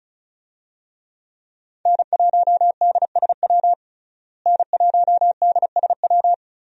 Morse code, heard as a keyed tone, about 35 wpm, 700 Hz. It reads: N1DSW N1DSW
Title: USA Call Signs - 5 Characters - Repeated 3x - 35wpm